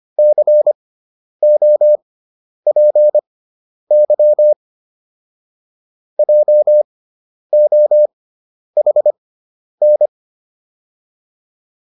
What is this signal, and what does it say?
Morse code, heard as a keyed tone, about 25 wpm, spaced as for 10 wpm, 600 Hz. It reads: COPY JOHN